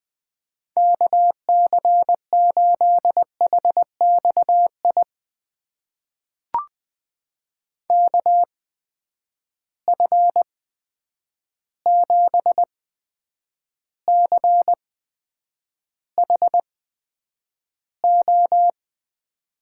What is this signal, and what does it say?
Morse code, heard as a keyed tone, about 20 wpm, spaced as for 9 wpm, 700 Hz. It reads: KC8HXI KF7CHO
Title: USA Call Signs - 6 Characters - 8x Character Spacing - 20wpm